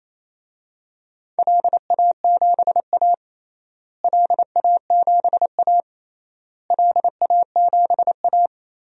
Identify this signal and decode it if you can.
Morse code, heard as a keyed tone, about 27 wpm, 700 Hz. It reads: LA7A LA7A LA7A
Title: DXCC-100 Call Signs - 4 Characters - Repeated 3x - 28wpm